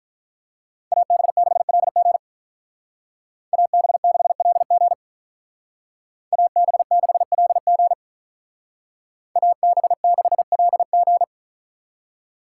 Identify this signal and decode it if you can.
Morse code, heard as a keyed tone, about 44 wpm, 700 Hz. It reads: AB6LG AB6LG AB6LG AB6LG